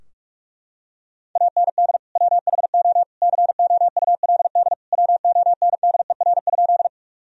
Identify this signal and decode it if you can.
Morse code, heard as a keyed tone, about 42 wpm, 700 Hz. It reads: AND WHO COULD WONDER?
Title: Sentences from Top 1000 Words - Review Part 3 - 45wpm